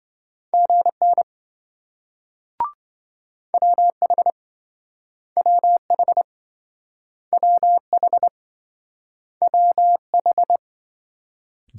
Morse code, heard as a keyed tone, about 27 wpm, 700 Hz. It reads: GN WH WH WH WH